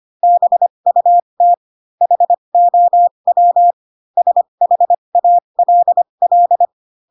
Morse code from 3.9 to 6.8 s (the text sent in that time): SHALL